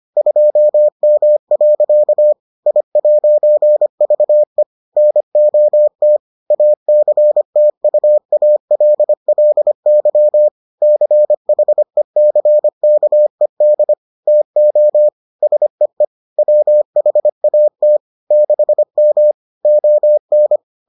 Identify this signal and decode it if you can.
Morse code, heard as a keyed tone, about 25 wpm, 600 Hz. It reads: 2M. I'VE NOT ACTUALLY CHECKED TO SEE WHAT 6M ON